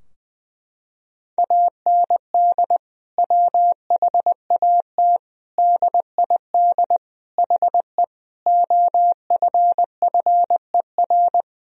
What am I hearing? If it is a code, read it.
Morse code, heard as a keyed tone, about 20 wpm, 700 Hz. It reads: AND WHAT DID HE OFFER
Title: Sentences from Top 1000 Words - Review Part 4 - 20wpm